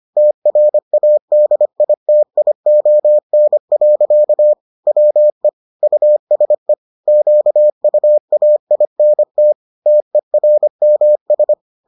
Morse code, heard as a keyed tone, about 25 wpm, 600 Hz. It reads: TRADITION. WE USE QUAINT TERMS